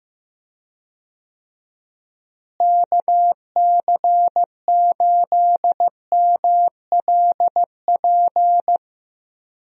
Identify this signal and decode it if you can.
Morse code, heard as a keyed tone, about 15 wpm, 700 Hz. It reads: KC8MLP
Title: USA Call Signs - 6 Characters - Repeated 3x - 15wpm